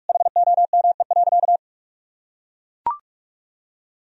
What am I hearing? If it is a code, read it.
Morse code, heard as a keyed tone, about 41 wpm, 700 Hz. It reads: HOME.
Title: Sentences from Top 1000 Words - Review Part 2 - 45wpm